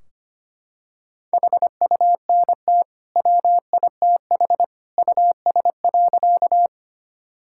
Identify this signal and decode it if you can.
Morse code, heard as a keyed tone, about 25 wpm, 700 Hz. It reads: HUNT WITH US.
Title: Sentences from Top 1600 Words 25wpm